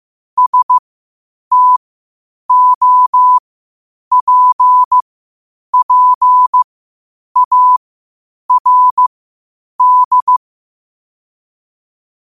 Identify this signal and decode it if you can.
Morse code, heard as a keyed tone, about 15 wpm, spaced as for 8 wpm, 1000 Hz. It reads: STOPPARD
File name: qotdfiles / 8wpm